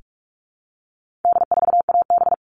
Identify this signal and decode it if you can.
Morse code, heard as a keyed tone, about 44 wpm, 700 Hz. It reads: D4AB
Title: All Cut Numbers - 4-Digit 45wpm